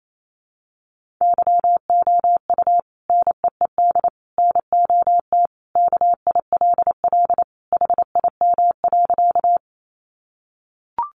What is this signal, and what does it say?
Morse code, heard as a keyed tone, about 28 wpm, 700 Hz. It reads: YOU NEED NOT KILL HIM.
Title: Sentences from Top 1400 Words 28wpm